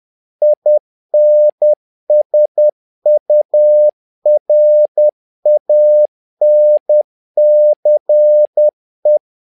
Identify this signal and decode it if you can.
Morse code, heard as a keyed tone, about 10 wpm, 600 Hz. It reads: INSURANCE
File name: learning center / tower trouble / tower trouble 10wpm